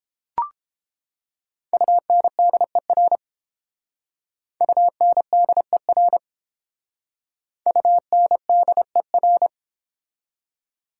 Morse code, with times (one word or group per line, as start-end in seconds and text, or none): 1.7-3.2 UNDER
4.6-6.2 UNDER
7.7-9.5 UNDER